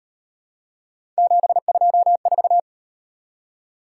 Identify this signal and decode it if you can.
Morse code, heard as a keyed tone, about 38 wpm, 700 Hz. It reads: 724